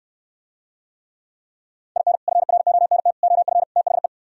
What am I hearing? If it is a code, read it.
Morse code, heard as a keyed tone, about 69 wpm, 700 Hz. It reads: IN FRONT OF THE